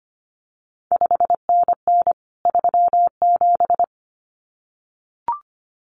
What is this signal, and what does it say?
Morse code, heard as a keyed tone, about 25 wpm, 700 Hz. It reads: 5NN 37
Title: CQ WW DX Contest Exchange Encore - 25wpm